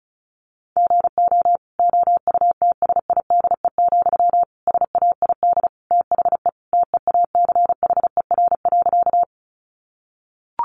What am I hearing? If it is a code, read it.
Morse code, heard as a keyed tone, about 34 wpm, 700 Hz. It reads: GO OUTSIDE, SAID THE TEACHER.
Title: Sentences from Top 900 Words 35wpm